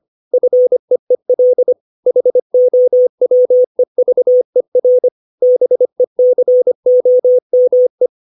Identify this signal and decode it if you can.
Morse code, heard as a keyed tone, about 25 wpm, 500 Hz. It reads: FEEL HOWEVER BECOME
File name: 632 words / 632 Words 25WPM